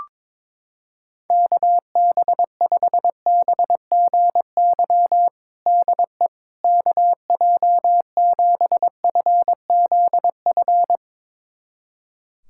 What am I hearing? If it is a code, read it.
Morse code, heard as a keyed tone, about 22 wpm, 700 Hz. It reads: KB5BGY DE KJ7FZF
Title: USA Callsign DE Callsign 22wpm